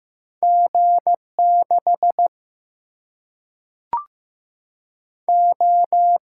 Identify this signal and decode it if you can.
Morse code, heard as a keyed tone, about 15 wpm, 700 Hz. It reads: G 6 O
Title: Single-Letter Single-Number 15wpm